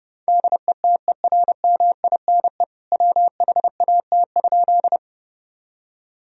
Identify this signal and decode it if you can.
Morse code, heard as a keyed tone, about 30 wpm, 700 Hz. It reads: DETERMINE WHAT?
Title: Sentences from Top 1000 Words - Review Part 4 - 30wpm